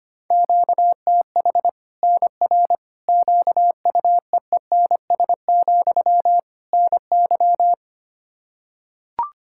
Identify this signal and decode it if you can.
Morse code, heard as a keyed tone, about 25 wpm, 700 Hz. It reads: QTH NR QUEENS, NY
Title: Practice QSO - Exchange 1 - Beginning - 25wpm